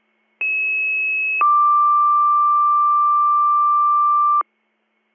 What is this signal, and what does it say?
Two-tone page, heard as a one-second tone, then a three-second tone, about 2600 then 1180 Hz. It is heard from 0.4 to 4.4 s.